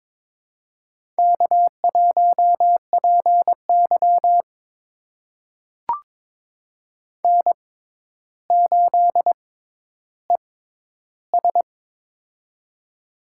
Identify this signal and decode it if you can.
Morse code, heard as a keyed tone, about 22 wpm, spaced as for 11 wpm, 700 Hz. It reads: K1PY N8ES
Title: USA Call Signs - 4 Characters - 6x Character Spacing - 22wpm